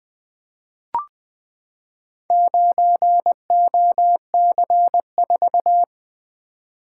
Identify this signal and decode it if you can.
Morse code, heard as a keyed tone, about 21 wpm, 700 Hz. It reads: 9OC4
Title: Four Letter-Number Combinations - 20wpm